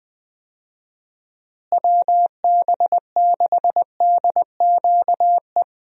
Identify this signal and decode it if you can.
Morse code, heard as a keyed tone, about 20 wpm, 700 Hz. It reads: WB6DQE
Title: USA Call Signs - 6 Characters - Repeated 2x - 20wpm